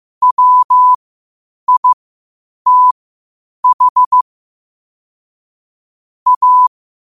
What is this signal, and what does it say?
Morse code, heard as a keyed tone, about 15 wpm, spaced as for 8 wpm, 1000 Hz. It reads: WITH A
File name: qotdfiles / 8wpm